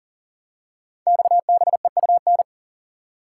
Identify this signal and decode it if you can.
Morse code, heard as a keyed tone, about 42 wpm, 700 Hz. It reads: XBEUN
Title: Code Groups - 5 Letters - 40wpm